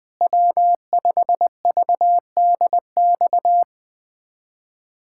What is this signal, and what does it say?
Morse code, heard as a keyed tone, about 20 wpm, 700 Hz. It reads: W5VDX
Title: USA Callsign DE Callsign 20wpm